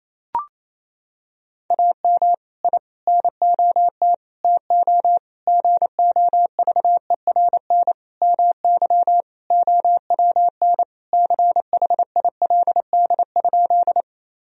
Morse code, heard as a keyed tone, about 28 wpm, 700 Hz. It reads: AM I NOT TO GOVERN MY OWN CHILD?
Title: Sentences from Top 1000 Words - Review Part 2 - 28wpm